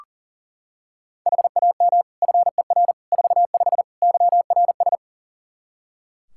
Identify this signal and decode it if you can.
Morse code, heard as a keyed tone, about 40 wpm, 700 Hz. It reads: HAM FER 45 YRS
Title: K1USN SST QSO Practice 40wpm